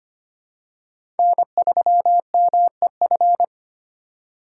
Morse code, heard as a keyed tone, about 25 wpm, 700 Hz. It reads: N3MEF
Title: USA Call Signs - 5 Characters - 4x Character Spacing - 25wpm